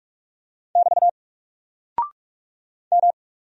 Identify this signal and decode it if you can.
Morse code, heard as a keyed tone, about 46 wpm, 700 Hz. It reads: <BT> M
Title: Single Character (Letter:Number:Punctuation:Prosign) Alt 45wpm